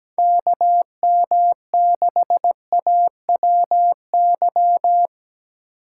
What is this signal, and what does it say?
Morse code, heard as a keyed tone, about 17 wpm, 700 Hz. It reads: KM6AWY
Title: USA Call Signs - 6 Characters - 6x Character Spacing - 17wpm